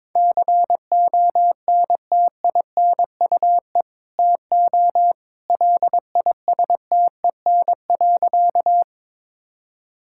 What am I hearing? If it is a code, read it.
Morse code, heard as a keyed tone, about 22 wpm, 700 Hz. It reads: CONTINUE TO LISTEN.
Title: Sentences from Top 1000 Words - Review Part 6 - 22wpm